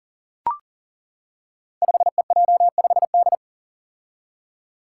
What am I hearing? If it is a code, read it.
Morse code, heard as a keyed tone, about 40 wpm, 700 Hz. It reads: 5EJ5D
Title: Five Challenging Letter-Number Combinations 40wpm